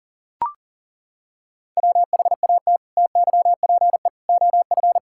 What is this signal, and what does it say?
Morse code, heard as a keyed tone, about 37 wpm, 700 Hz. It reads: WHAT TYPE OF
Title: Sentences from Top 1000 Words - Review Part 5 - 40wpm